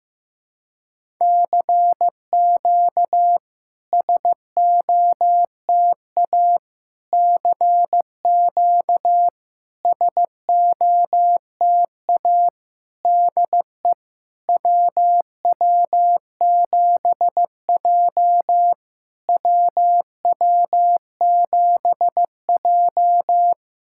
Morse code, heard as a keyed tone, about 15 wpm, 690 Hz. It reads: CQ SOTA CQ SOTA DE WW7J WW7J
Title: SOTA QSO Practice 15wpm